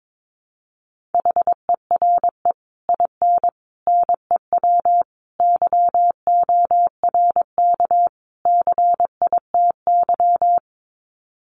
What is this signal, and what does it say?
Morse code, heard as a keyed tone, about 22 wpm, 700 Hz. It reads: HERE IN NEW YORK CITY
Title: Sets of 5 Words Encore2 22wpm